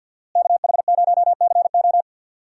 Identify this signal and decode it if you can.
Morse code, heard as a keyed tone, about 53 wpm, 700 Hz. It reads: KH0CO
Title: USA Call Signs - 5 Characters - Repeated 2x - 50wpm